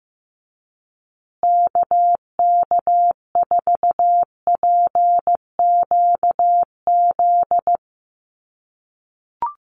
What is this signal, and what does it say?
Morse code, heard as a keyed tone, about 15 wpm, 700 Hz. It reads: KK4PQZ
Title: USA Call Signs Repeated 2x 15wpm